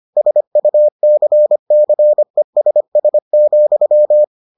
Morse code, heard as a keyed tone, about 25 wpm, 600 Hz. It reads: SUCCESS,